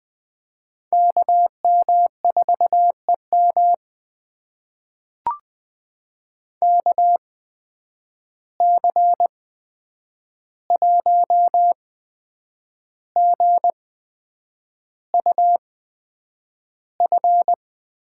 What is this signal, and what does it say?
Morse code, heard as a keyed tone, about 20 wpm, spaced as for 8 wpm, 700 Hz. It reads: KM4EM KC1GUF
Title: USA Call Signs - 8x Character Spacing - 20wpm